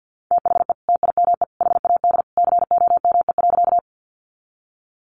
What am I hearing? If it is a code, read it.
Morse code, heard as a keyed tone, about 50 wpm, 700 Hz. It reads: THE TIME HAD COME.
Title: Sentences from Top 100 Words 50wpm